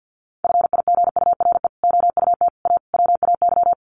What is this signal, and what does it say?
Morse code, heard as a keyed tone, about 48 wpm, 700 Hz. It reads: FIGURE OUT A WAY
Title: Sets of 5 Words Encore2 50wpm